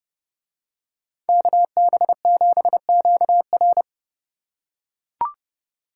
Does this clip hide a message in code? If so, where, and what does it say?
Yes: Morse code from 1.3 to 3.8 s, heard as a keyed tone, about 30 wpm, 700 Hz. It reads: KB7QR